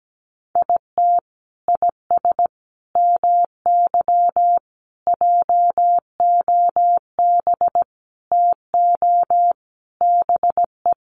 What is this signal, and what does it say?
Morse code, heard as a keyed tone, about 17 wpm, 700 Hz. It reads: IT IS MY JOB TO BE
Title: Sentences from Top 600 Words 17wpm